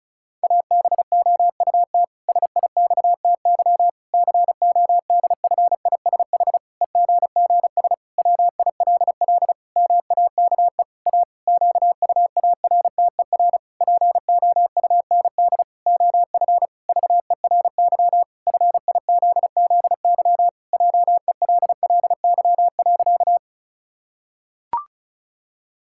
Morse code, heard as a keyed tone, about 36 wpm, 700 Hz. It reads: ABOUT SIXTY CODFISH EGGS WILL MAKE A QUARTER POUND OF VERY FIZZY JELLY.
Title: Pangram Challenge v2 - 35wpm